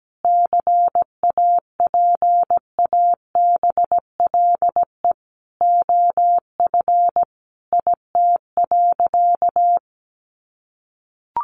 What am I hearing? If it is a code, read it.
Morse code, heard as a keyed tone, about 17 wpm, 700 Hz. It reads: CAPABLE OF IT.
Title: Sentences from Top 1200 Words 17wpm